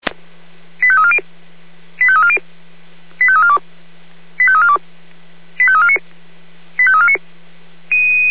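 Five-tone selective call (ZVEI), heard as quick runs of five tones, tones of about 80 ms. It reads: 85340 85340 85342 85342 85349 85349
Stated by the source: ZVEI five-tone selective call received on an SDR.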